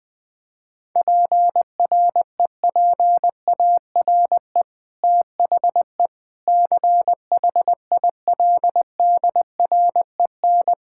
Morse code, heard as a keyed tone, about 20 wpm, 700 Hz. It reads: PREPARE THE CHILDREN
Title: Sentences from Top 1000 Words - Review Part 6 - 20wpm